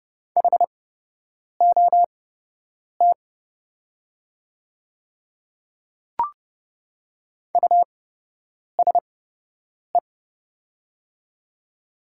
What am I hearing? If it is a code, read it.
Morse code, heard as a keyed tone, about 29 wpm, spaced as for 8 wpm, 700 Hz. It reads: HOT USE
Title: 8x Character Spacing - Top 100 Words - 30wpm